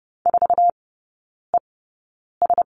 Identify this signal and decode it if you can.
Morse code, heard as a keyed tone, about 30 wpm, spaced as for 9 wpm, 700 Hz. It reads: 4ES
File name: 081 Lesson 14e - Number 2 - Three Characters 30wpm